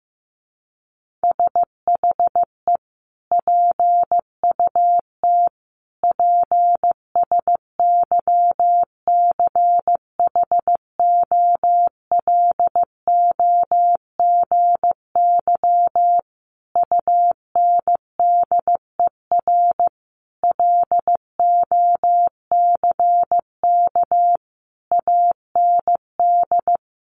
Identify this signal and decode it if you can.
Morse code, heard as a keyed tone, about 15 wpm, 700 Hz. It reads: SHE PUT PSYCHOLOGY UNDER LOCK AND
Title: Sentences from Top 1400 Words 15wpm